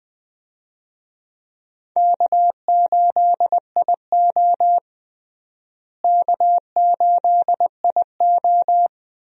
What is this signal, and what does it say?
Morse code, heard as a keyed tone, about 20 wpm, 700 Hz. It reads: K8IO K8IO
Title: USA Call Signs - 4 Characters - Repeated 2x - 20wpm